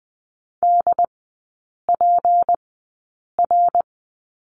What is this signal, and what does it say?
Morse code, heard as a keyed tone, about 20 wpm, spaced as for 8 wpm, 700 Hz. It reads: DPR
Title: 178 Lesson 29e - Slash - Three Characters 20wpm